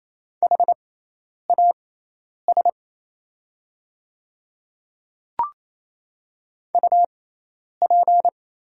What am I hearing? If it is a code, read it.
Morse code, heard as a keyed tone, about 28 wpm, spaced as for 10 wpm, 700 Hz. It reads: HAS UP
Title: 6x Character Spacing - Top 300 Words - 28wpm